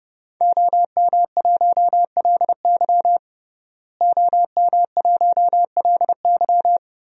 Morse code, heard as a keyed tone, about 30 wpm, 700 Hz. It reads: OM1LY OM1LY
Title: DXCC-100 Call Signs - 5 Characters - Repeated 3x - 30wpm